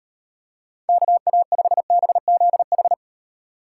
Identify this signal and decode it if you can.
Morse code, heard as a keyed tone, about 38 wpm, 700 Hz. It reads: KA5BZH